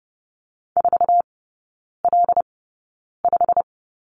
Morse code, heard as a keyed tone, about 30 wpm, spaced as for 9 wpm, 700 Hz. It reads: 4L5
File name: course / 30:10wpm / 087 Lesson 15e - Number 5 - Three Characters 30wpm